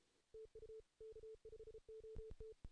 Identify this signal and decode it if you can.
Morse code, heard as a keyed tone, about 21 wpm, 440 Hz. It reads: TU K5OT
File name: 2016 JAN CW NAQP / 2016 Jan naqp k5ot - 0-3